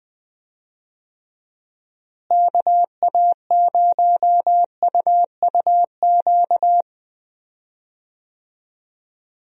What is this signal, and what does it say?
Morse code, heard as a keyed tone, about 20 wpm, 700 Hz. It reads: KA0UUQ
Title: USA Call Signs - 6 Characters - Repeated 3x - 20wpm